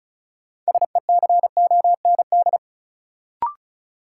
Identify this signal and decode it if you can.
Morse code, heard as a keyed tone, about 34 wpm, 700 Hz. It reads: SECOND
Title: Sentences from Top 300 Words - New Word Drill - 35wpm